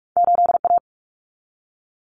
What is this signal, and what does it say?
Morse code, heard as a keyed tone, about 46 wpm, 700 Hz. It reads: 8A
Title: Common Cut Numbers - 2-Digit 45wpm